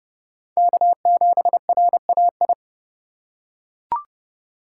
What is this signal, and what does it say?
Morse code, heard as a keyed tone, about 28 wpm, 700 Hz. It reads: K7RAI